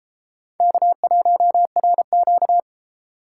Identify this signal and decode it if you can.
Morse code, heard as a keyed tone, about 31 wpm, 700 Hz. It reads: K1RQ